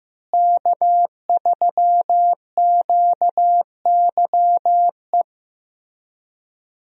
Morse code, heard as a keyed tone, about 15 wpm, 700 Hz. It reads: K3QYE
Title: USA Call Signs - 5 Characters - Repeated 3x - 15wpm